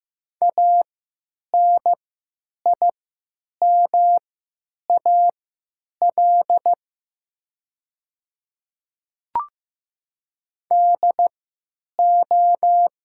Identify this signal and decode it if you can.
Morse code, heard as a keyed tone, about 15 wpm, spaced as for 9 wpm, 700 Hz. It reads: ANIMAL DO
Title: 3x Character Spacing - Top 200 Words - 15wpm